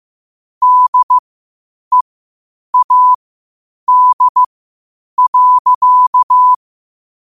Morse code, heard as a keyed tone, about 15 wpm, spaced as for 8 wpm, 1000 Hz. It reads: DEAD.
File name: qotdfiles / 8wpm